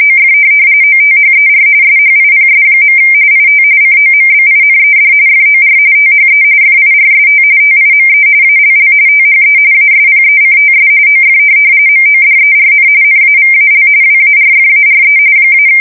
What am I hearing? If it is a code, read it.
RTTY, about 45.45 baud. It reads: OF OTHER SHIPS. IT COMBINED FOR THE FIRST TIME ALL THE ASPECTS OF MODERN NAVAL WARFARE -
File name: fsk170 / USS-New-Jersey